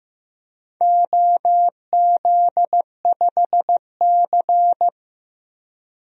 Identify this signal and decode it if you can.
Morse code, heard as a keyed tone, about 15 wpm, 700 Hz. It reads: OZ5C
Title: DXCC-100 Call Signs - 4 Characters - Repeated 2x - 15wpm